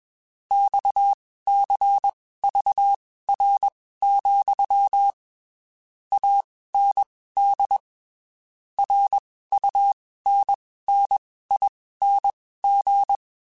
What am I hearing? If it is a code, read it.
Morse code, heard as a keyed tone, about 21 wpm, spaced as for 15 wpm, 800 Hz. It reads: XCVR, AND RUNNING
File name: (spaced) / QSOs Spaced-mp3s / WD0FBT 1 @20 sp